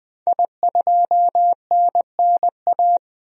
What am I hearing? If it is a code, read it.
Morse code, heard as a keyed tone, about 20 wpm, 700 Hz. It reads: I2NNA
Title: DXCC-100 Call Signs - 5 Characters - 20wpm